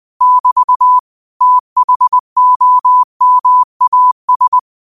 Morse code, heard as a keyed tone, about 20 wpm, 1000 Hz. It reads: <BT> THOMAS
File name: qotdfiles / 20wpm